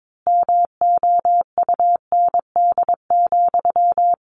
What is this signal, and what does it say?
Morse code, heard as a keyed tone, about 22 wpm, 700 Hz. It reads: MOUND,
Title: A Sound Of Thunder 22wpm